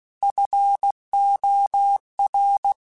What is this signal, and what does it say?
Morse code, heard as a keyed tone, about 16 wpm, 800 Hz. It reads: FOR
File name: QSOs / QSOs-mp3s / WD0FBT 1@15